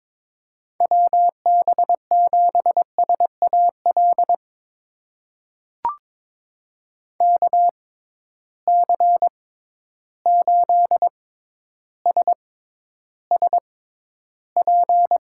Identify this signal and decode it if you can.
Morse code, heard as a keyed tone, about 23 wpm, spaced as for 11 wpm, 700 Hz. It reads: WB7SAL KC8SSP
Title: USA Call Signs - 6 Characters - 6x Character Spacing - 22wpm